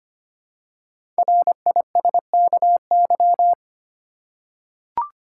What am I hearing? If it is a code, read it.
Morse code, heard as a keyed tone, about 25 wpm, 700 Hz. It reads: RISKY
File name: 5 Letter Words - First 3 Letters Given - 25wpm